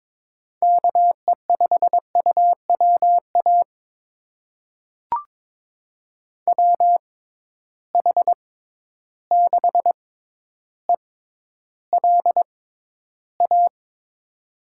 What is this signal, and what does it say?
Morse code, heard as a keyed tone, about 21 wpm, spaced as for 11 wpm, 700 Hz. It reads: KE5UWA WH6ELA
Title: USA Call Signs - 6 Characters - 6x Character Spacing - 22wpm